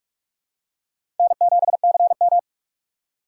Morse code, heard as a keyed tone, about 46 wpm, 700 Hz. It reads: N7CM
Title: USA Call Signs - 4 Characters - 2x Character Spacing - 45wpm